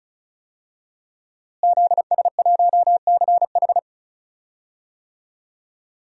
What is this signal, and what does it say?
Morse code, heard as a keyed tone, about 36 wpm, 700 Hz. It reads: ZS1CH
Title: DXCC-100 Call Signs - 5 Characters - Repeated 3x - 35wpm